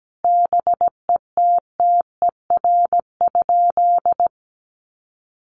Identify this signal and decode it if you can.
Morse code, heard as a keyed tone, about 17 wpm, 700 Hz. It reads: BETTER?
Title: Sentences from Top 400 Words 17wpm